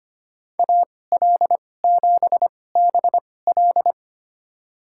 Morse code, heard as a keyed tone, about 25 wpm, spaced as for 18 wpm, 700 Hz. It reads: AL7BL
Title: USA Call Signs - 5 Characters - 2x Character Spacing - 25wpm